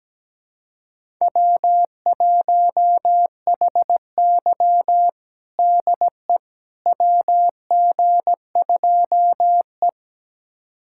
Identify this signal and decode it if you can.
Morse code, heard as a keyed tone, about 17 wpm, 700 Hz. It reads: W1HY DE WG2E
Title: USA Callsign DE Callsign - 4 Characters - 17wpm